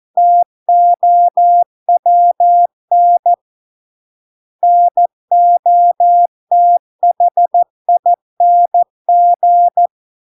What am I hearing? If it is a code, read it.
Morse code, heard as a keyed tone, about 14 wpm, 700 Hz. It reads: TOWN NOTHING